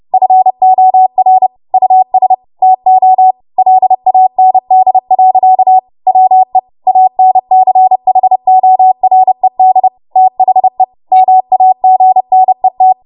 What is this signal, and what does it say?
Morse code, heard as a keyed tone, about 30 wpm, 750 Hz. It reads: FOR US TO LAND. WE ANCHORED THE MAGNET
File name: Morse / 30WPM